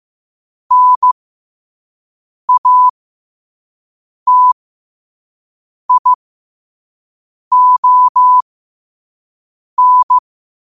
Morse code, heard as a keyed tone, about 15 wpm, spaced as for 5 wpm, 1000 Hz. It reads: NATION
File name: qotdfiles / 5wpm